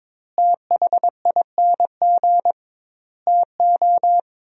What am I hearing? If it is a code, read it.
Morse code, heard as a keyed tone, about 22 wpm, 700 Hz. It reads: THING TO
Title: 2x Word-Spacing - Sentences from Top 100 Words - 22wpm